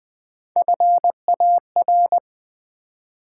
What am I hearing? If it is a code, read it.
Morse code, heard as a keyed tone, about 20 wpm, 700 Hz. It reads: FAR
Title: Three Letter Words - 20wpm